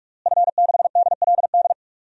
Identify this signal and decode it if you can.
Morse code, heard as a keyed tone, about 46 wpm, 700 Hz. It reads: F6DLD
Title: DXCC-100 Call Signs - 5 Characters - Repeated 3x - 45wpm